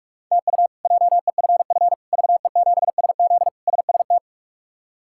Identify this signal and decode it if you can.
Morse code, heard as a keyed tone, about 46 wpm, 700 Hz. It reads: TU JEFF VE7SZ SST
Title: K1USN SST QSO Practice - 45wpm